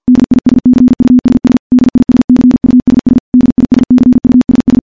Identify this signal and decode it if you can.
Morse code, heard as a keyed tone, about 52 wpm, 250 Hz. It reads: DISCUSS DISCUSS DISCUSS
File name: morse / morse-training-50WPM